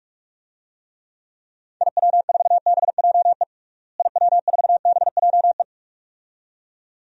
Morse code, heard as a keyed tone, about 41 wpm, 700 Hz. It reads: IW4BJE IW4BJE
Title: DXCC-100 Call Signs - 6 Characters - Repeated 2x - 45wpm